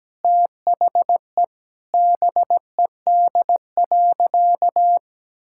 Morse code, heard as a keyed tone, about 17 wpm, 700 Hz. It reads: THE BED.
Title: Sentences from Top 1000 Words - Review Part 6 - 17wpm